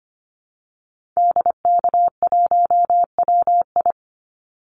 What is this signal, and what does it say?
Morse code, heard as a keyed tone, about 25 wpm, 700 Hz. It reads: DK1WI
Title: Call Signs - 25wpm